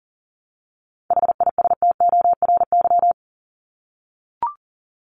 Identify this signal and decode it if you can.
Morse code, heard as a keyed tone, about 40 wpm, 700 Hz. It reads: HISTORY
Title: Top 501-750 Words 40wpm